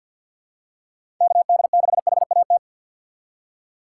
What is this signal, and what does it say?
Morse code, heard as a keyed tone, about 53 wpm, 700 Hz. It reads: KD6HAT
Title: USA Call Signs - 6 Characters - 4x Character Spacing - 50wpm